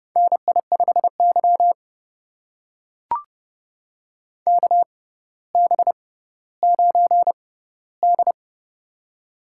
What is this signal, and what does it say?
Morse code, heard as a keyed tone, about 32 wpm, spaced as for 16 wpm, 700 Hz. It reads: NI5Y KB9D
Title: USA Call Signs - 4 Characters - 6x Character Spacing - 30wpm